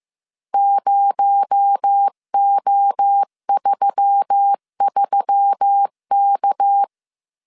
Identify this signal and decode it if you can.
Morse code, heard as a keyed tone, about 15 wpm, 800 Hz. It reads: 0O33K
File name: CW / DET / 1 6 1500C S C 14wpm 2951